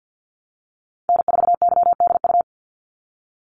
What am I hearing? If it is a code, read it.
Morse code, heard as a keyed tone, about 50 wpm, 700 Hz. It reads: N4YDU
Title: Call Signs - 50wpm